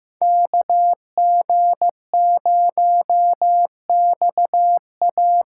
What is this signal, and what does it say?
Morse code, heard as a keyed tone, about 15 wpm, 700 Hz. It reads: KG0XA
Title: USA Call Signs - 5 Characters - 2x Character Spacing - 15wpm